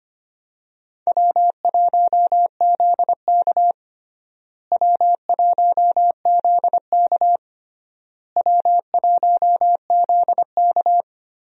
Morse code, heard as a keyed tone, about 25 wpm, 700 Hz. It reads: W1ZK W1ZK W1ZK
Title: USA Call Signs - 4 Characters - Repeated 3x - 25wpm